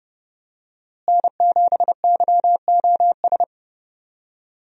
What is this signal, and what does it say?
Morse code, heard as a keyed tone, about 29 wpm, 700 Hz. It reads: N7YOS
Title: USA Call Signs - 5 Characters - Repeated 2x - 30wpm